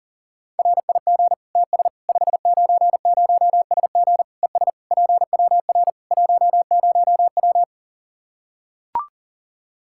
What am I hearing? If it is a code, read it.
Morse code, heard as a keyed tone, about 40 wpm, 700 Hz. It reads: RIG TS 590SG ES PWR 10W
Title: Practice QSO - Exchange 2 - Info - 40wpm